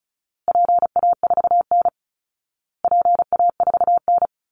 Morse code, heard as a keyed tone, about 34 wpm, 700 Hz. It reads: PA4N PA4N
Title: Call Signs Repeated 2x - 35wpm